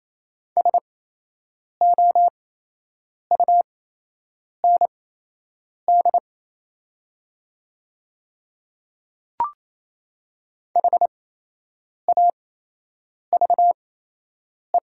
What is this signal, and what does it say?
Morse code, heard as a keyed tone, about 28 wpm, spaced as for 8 wpm, 700 Hz. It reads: SOUND HAVE
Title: 8x Character Spacing - Top 100 Words - 28wpm